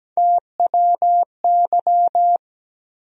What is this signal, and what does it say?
Morse code, heard as a keyed tone, about 17 wpm, 700 Hz. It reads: TWY
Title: USA Call Sign Postfixes - 3 Letters 17wpm